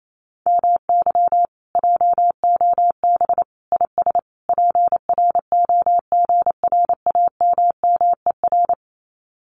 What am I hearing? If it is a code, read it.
Morse code, heard as a keyed tone, about 28 wpm, 700 Hz. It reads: MY JOB IS PROGRAMMER
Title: Common QSO Phrases Encore 28wpm